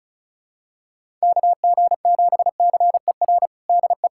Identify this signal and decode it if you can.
Morse code, heard as a keyed tone, about 34 wpm, 700 Hz. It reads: KG7CER DE
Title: USA Callsign DE Callsign 35wpm